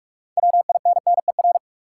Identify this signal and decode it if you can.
Morse code, heard as a keyed tone, about 46 wpm, 700 Hz. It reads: WINNER
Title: Six Letter Words - 45wpm